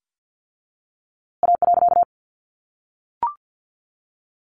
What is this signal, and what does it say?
Morse code, heard as a keyed tone, about 50 wpm, 700 Hz. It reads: A.
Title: A Sound Of Thunder 50wpm